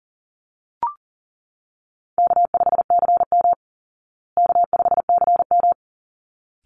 Morse code, heard as a keyed tone, about 40 wpm, 700 Hz. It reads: K5CM K5CM
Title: Call Signs Repeated 3x - 40wpm